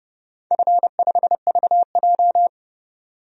Morse code, heard as a keyed tone, about 30 wpm, 700 Hz. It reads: F5VJ
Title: DXCC-100 Call Signs - 4 Characters - 30wpm